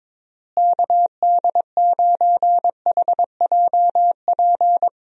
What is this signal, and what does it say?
Morse code, heard as a keyed tone, about 22 wpm, 700 Hz. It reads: KD9HJP
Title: USA Call Signs - 6 Characters - Repeated 2x - 22wpm